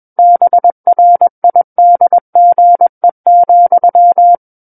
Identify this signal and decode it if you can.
Morse code, heard as a keyed tone, about 21 wpm, 700 Hz. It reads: BRIDGE,
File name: Book One / 12-Chapter 12 - 21 WPM